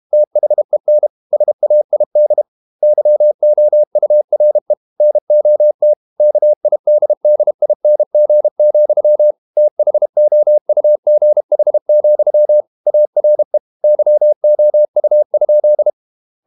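Morse code, heard as a keyed tone, about 32 wpm, 600 Hz. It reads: THEN SAID YOURE NOT KIDDING, THOUGH, ARE YOU?
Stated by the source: Morse code practice at 32 wpm from CW med Gnister, reading Ken Follett's novel Never in English.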